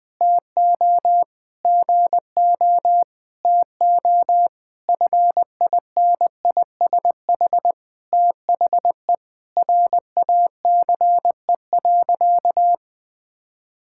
Morse code, heard as a keyed tone, about 20 wpm, 700 Hz. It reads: TO GO TO FINISH THE RACE.
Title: Sentences with 8-Word Phrases - 20wpm